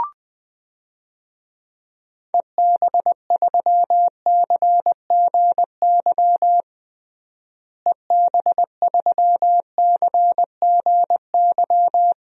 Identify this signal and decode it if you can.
Morse code, heard as a keyed tone, about 20 wpm, 700 Hz. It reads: EB3CGY EB3CGY
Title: DXCC-100 Call Signs - 6 Characters - Repeated 3x - 20wpm